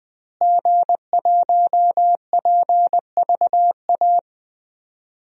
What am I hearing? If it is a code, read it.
Morse code, heard as a keyed tone, about 20 wpm, 700 Hz. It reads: G1PVA
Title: DXCC-100 Call Signs - 5 Characters - Repeated 2x - 20wpm